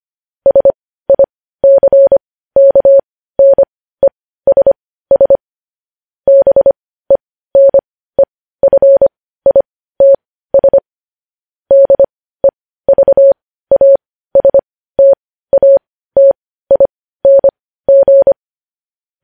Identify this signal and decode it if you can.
Morse code, heard as a keyed tone, about 25 wpm, spaced as for 15 wpm, 550 Hz. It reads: SICKNESS BENEFITS DEVASTATING
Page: news-headlines-in-morse-code-at-15-wpm